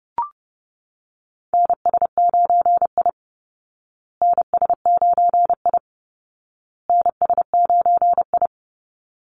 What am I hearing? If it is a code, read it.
Morse code, heard as a keyed tone, about 29 wpm, 700 Hz. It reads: NS9I NS9I NS9I
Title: Call Signs Repeated 2x - 30wpm